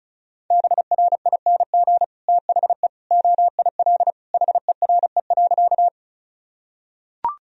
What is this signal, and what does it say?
Morse code, heard as a keyed tone, about 34 wpm, 700 Hz. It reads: BRING THE OIL HERE.
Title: Sentences from Top 1000 Words - Review Part 4 - 35wpm